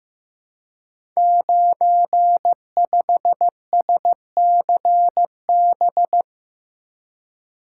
Morse code, heard as a keyed tone, about 15 wpm, 700 Hz. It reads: 95SCB
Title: Code Groups - 5 Letter:Numbers - 15wpm